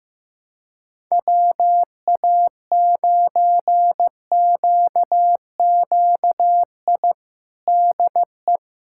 Morse code, heard as a keyed tone, about 15 wpm, 700 Hz. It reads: WA9QQI DE
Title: USA Callsign DE Callsign 15wpm